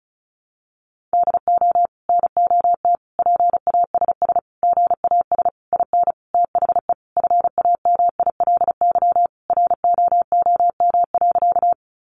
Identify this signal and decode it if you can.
Morse code, heard as a keyed tone, about 34 wpm, 700 Hz. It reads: DO NOT PASS GAS IN THE FAMILY ROOM.
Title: Sentences from Top 1100 Words 35wpm